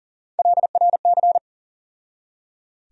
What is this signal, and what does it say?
Morse code, heard as a keyed tone, about 40 wpm, 700 Hz. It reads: LRC
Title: USA Call Sign Postfixes - 3 Letters 40wpm